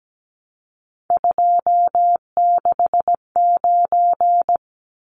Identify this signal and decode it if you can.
Morse code, heard as a keyed tone, about 17 wpm, 700 Hz. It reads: 269
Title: Numbers - 3-Digit 17wpm